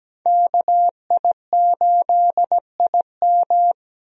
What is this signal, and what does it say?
Morse code, heard as a keyed tone, about 17 wpm, 700 Hz. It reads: KI8IM
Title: USA Call Signs - 5 Characters - 17wpm